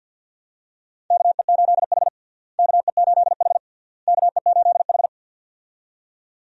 Morse code, heard as a keyed tone, about 53 wpm, 700 Hz. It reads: KE8H KE8H KE8H
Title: USA Call Signs - 4 Characters - Repeated 3x - 50wpm